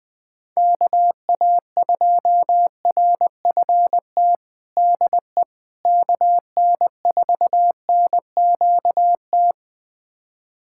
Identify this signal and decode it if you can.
Morse code, heard as a keyed tone, about 20 wpm, 700 Hz. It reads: KA2RFT DE KN4NQT
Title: USA Callsign DE Callsign 20wpm